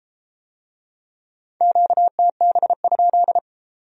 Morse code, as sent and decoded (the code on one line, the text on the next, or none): --.- - -... ..--..
QTB?